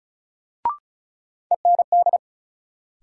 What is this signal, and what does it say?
Morse code, heard as a keyed tone, about 35 wpm, 700 Hz. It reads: END
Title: Sentences from Top 200 Words - New Word Drill - 35wpm